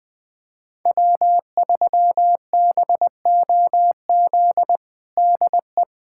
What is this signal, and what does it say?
Morse code, heard as a keyed tone, about 21 wpm, 700 Hz. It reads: W3BOZ DE
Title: USA Callsign DE Callsign - 5 Characters - 20wpm